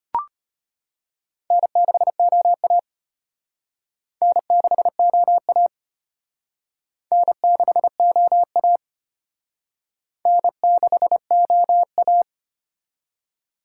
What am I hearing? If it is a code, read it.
Morse code, heard as a keyed tone, about 31 wpm, 700 Hz. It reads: N6OA N6OA N6OA N6OA
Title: Speed Racing - USA 1x2 Call Signs 25wpm